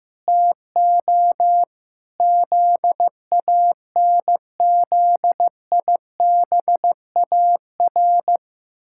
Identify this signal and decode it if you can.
Morse code, heard as a keyed tone, about 16 wpm, 700 Hz. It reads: TO ZANZIBAR
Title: Pangram Challenge v2 - 15wpm